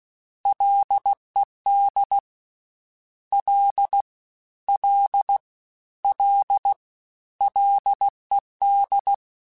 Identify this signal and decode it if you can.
Morse code, heard as a keyed tone, about 16 wpm, 800 Hz. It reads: LED L L L LED
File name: Ls / Single L m3s / Single L Prefixe 3x2 Twice @15